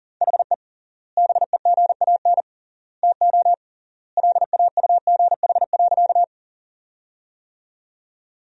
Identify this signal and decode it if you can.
Morse code, heard as a keyed tone, about 40 wpm, 700 Hz. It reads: HE BEGAN TO LAUGH.
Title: 3x Word-Spacing - Sentences from Top 500 Words - 40wpm